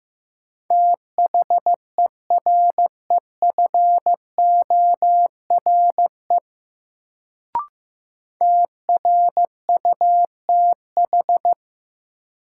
Morse code, heard as a keyed tone, about 15 wpm, 700 Hz. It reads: THEREFORE TRUTH
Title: Sentences from Top 800 Words - New Word Drill - 15wpm